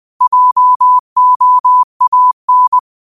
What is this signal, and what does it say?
Morse code, heard as a keyed tone, about 20 wpm, 1000 Hz. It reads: JOAN